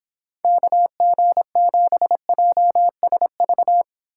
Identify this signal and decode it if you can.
Morse code, heard as a keyed tone, about 26 wpm, 700 Hz. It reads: KG7JSV